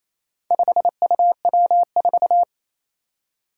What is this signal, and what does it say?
Morse code, heard as a keyed tone, about 28 wpm, 700 Hz. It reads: 5UW4
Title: Four Challenging Letter-Number Combinations 28wpm